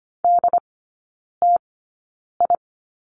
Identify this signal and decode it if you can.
Morse code, heard as a keyed tone, about 25 wpm, spaced as for 9 wpm, 700 Hz. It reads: DTI
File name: course / 25:10wpm / 093 Lesson 16e - Letter c - Three Characters 25wpm